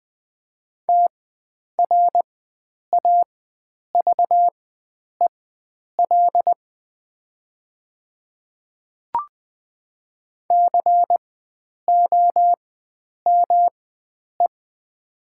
Morse code, heard as a keyed tone, about 20 wpm, spaced as for 9 wpm, 700 Hz. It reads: TRAVEL COME
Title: 4x Character Spacing - Top 500 Words - 20wpm